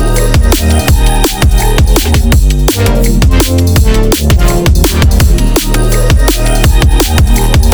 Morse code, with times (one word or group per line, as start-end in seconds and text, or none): none